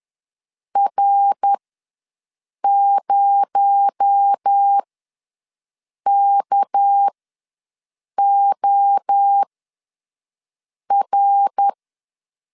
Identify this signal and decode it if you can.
Morse code, heard as a keyed tone, about 10 wpm, spaced as for 5 wpm, 800 Hz. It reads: R0KOR
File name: LEZIONI / 6 PKX0 / 6 PKX0 ws 10wpm 4220